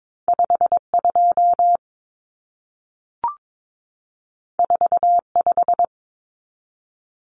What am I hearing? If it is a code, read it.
Morse code, heard as a keyed tone, about 22 wpm, 700 Hz. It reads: 52 45
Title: Numbers - 2-Digit 22wpm